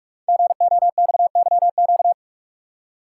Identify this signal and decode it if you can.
Morse code, heard as a keyed tone, about 44 wpm, 700 Hz. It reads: GOXYQ
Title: Code Groups - 5 Letter:Numbers - 45wpm